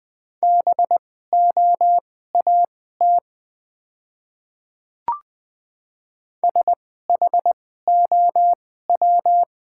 Morse code, heard as a keyed tone, about 20 wpm, spaced as for 14 wpm, 690 Hz. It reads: BOAT SHOW
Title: 2x Character Spacing - Top 500 Words - 20wpm